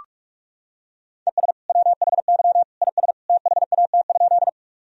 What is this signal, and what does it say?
Morse code, heard as a keyed tone, about 46 wpm, 700 Hz. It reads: ES WHY IS THAT?
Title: Sentences from Top 200 Words and Abbreviations 45wpm